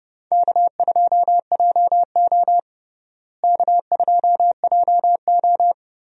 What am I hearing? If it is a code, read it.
Morse code, heard as a keyed tone, about 30 wpm, 700 Hz. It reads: K2JO K2JO
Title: USA Call Signs - 4 Characters - Repeated 2x - 30wpm